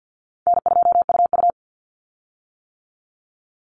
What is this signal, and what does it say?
Morse code, heard as a keyed tone, about 48 wpm, 700 Hz. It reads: N2UU
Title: Easy Call Signs - 50wpm